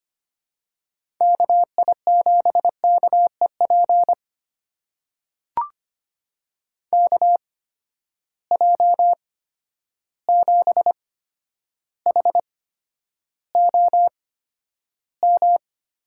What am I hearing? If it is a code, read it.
Morse code, heard as a keyed tone, about 25 wpm, spaced as for 11 wpm, 700 Hz. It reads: KI7KEP KJ7HOM
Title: USA Call Signs - 6 Characters - 8x Character Spacing - 25wpm